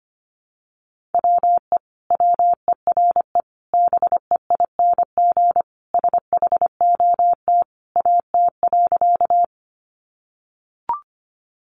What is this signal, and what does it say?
Morse code, heard as a keyed tone, about 25 wpm, 700 Hz. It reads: WE WERE BEING SHOT AT.
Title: Sentences from Top 1100 Words 25wpm